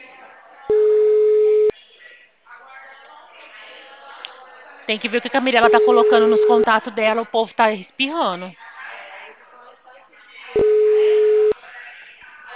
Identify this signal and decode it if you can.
Ringback tone (Europe, 425 Hz), with speech.
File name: out- -2046- - - .63128